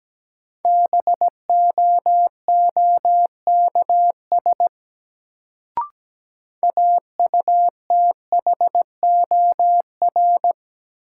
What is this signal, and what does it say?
Morse code, heard as a keyed tone, about 17 wpm, 700 Hz. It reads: BOOKS AUTHOR
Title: Sentences from Top 1200 Words - New Word Drill Alt - 17wpm